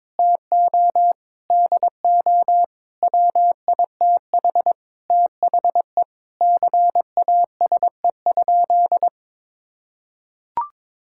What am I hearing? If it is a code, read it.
Morse code, heard as a keyed tone, about 22 wpm, 700 Hz. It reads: TO DO WITH THE CASE?
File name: practice / 22wpm / Sentences with 4-Word Phrases - 22wpm